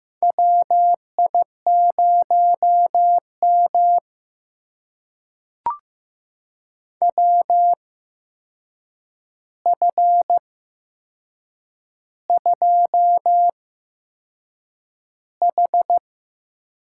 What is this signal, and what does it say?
Morse code, heard as a keyed tone, about 15 wpm, spaced as for 6 wpm, 700 Hz. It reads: WI0M WF2H
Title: USA Call Signs - 4 Characters - 8x Character Spacing - 15wpm